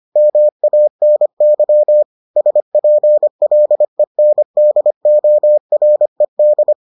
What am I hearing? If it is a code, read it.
Morse code, heard as a keyed tone, about 25 wpm, 600 Hz. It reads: MANY SPLENDORED